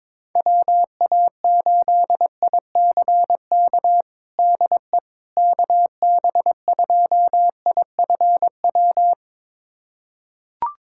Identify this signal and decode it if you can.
Morse code, heard as a keyed tone, about 21 wpm, 700 Hz. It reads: WA8ICK DE KB2IFW
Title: USA Callsign DE Callsign 22wpm